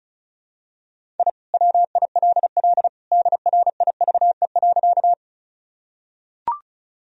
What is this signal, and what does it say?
Morse code, heard as a keyed tone, about 34 wpm, 700 Hz. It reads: I WILL DRIVE.
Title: Sentences from Top 1000 Words - Review Part 2 - 35wpm